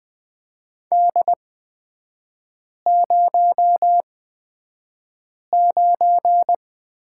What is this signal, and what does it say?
Morse code, heard as a keyed tone, about 20 wpm, spaced as for 5 wpm, 700 Hz. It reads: D09
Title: 236 Lesson 38e - Letter q - Three Characters 20wpm